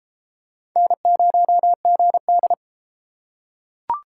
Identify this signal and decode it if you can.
Morse code, heard as a keyed tone, about 34 wpm, 700 Hz. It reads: N0GD